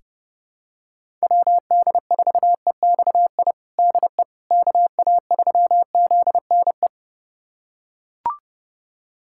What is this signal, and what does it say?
Morse code, heard as a keyed tone, about 28 wpm, 700 Hz. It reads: WD4EXI DE KA3ZNE
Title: USA Callsign DE Callsign - 6 Characters - 30wpm